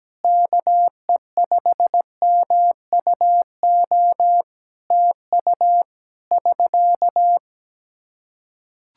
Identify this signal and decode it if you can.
Morse code, heard as a keyed tone, about 17 wpm, 700 Hz. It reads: KE5MUO TU <SK>
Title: Practice QSO - Exchange 4 - Ending - 17wpm